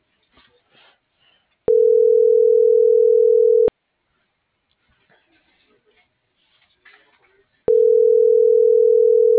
Ringback tone (North America), 440 + 480 Hz.